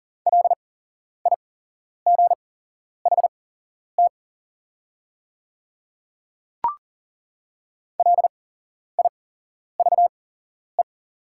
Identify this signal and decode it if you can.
Morse code, heard as a keyed tone, about 40 wpm, spaced as for 11 wpm, 700 Hz. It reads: LIGHT LIVE
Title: 8x Character Spacing - Top 300 Words - 40wpm